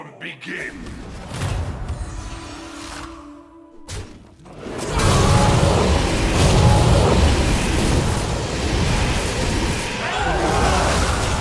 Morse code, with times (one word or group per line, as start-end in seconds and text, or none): none